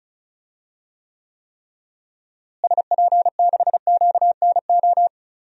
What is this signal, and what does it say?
Morse code, heard as a keyed tone, about 36 wpm, 700 Hz. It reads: SP6QNO
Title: DXCC-100 Call Signs - 6 Characters - Repeated 3x - 35wpm